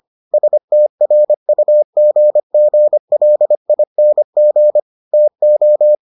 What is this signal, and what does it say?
Morse code, heard as a keyed tone, about 25 wpm, 600 Hz. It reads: STRUGGLING TO